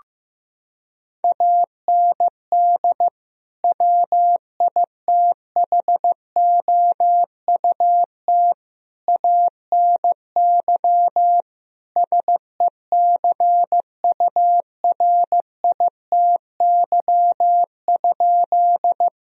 Morse code, heard as a keyed tone, about 15 wpm, 700 Hz. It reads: AND WITHOUT ANY SECURITY?
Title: Sentences from Top 1000 Words - Review Part 4 - 15wpm